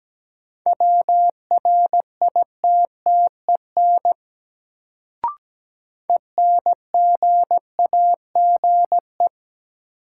Morse code, heard as a keyed tone, about 17 wpm, 700 Hz. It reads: WRITTEN ENGAGE
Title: Sentences from Top 1500 Words - New Word Drill - 17wpm